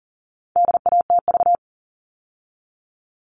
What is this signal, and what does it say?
Morse code, heard as a keyed tone, about 40 wpm, 700 Hz. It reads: DATV
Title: All Cut Numbers - 4-Digit 40wpm